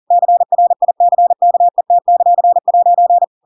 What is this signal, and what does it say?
Morse code, heard as a keyed tone, about 40 wpm, 700 Hz. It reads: CRICKET;'
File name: Pinocchio / Pinocchio ch7 40 40 ik1hge